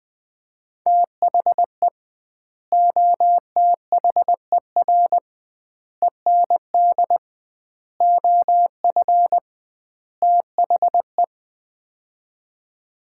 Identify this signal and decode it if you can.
Morse code, heard as a keyed tone, about 20 wpm, 700 Hz. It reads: THE OTHER END OF THE
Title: 2x Word-Spacing - Sets of 5 Words - 20wpm